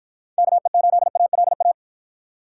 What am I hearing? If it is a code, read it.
Morse code, heard as a keyed tone, about 51 wpm, 700 Hz. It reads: KE8ALA